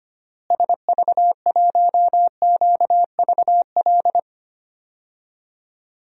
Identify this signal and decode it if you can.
Morse code, heard as a keyed tone, about 25 wpm, 700 Hz. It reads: SV1QVL
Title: DXCC-100 Call Signs - 6 Characters - Repeated 2x - 25wpm